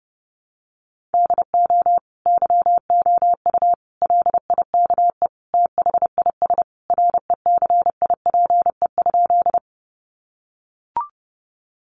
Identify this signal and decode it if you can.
Morse code, heard as a keyed tone, about 30 wpm, 700 Hz. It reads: DO YOU LIKE THIS RECIPE?
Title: Sentences from Top 1200 Words 30wpm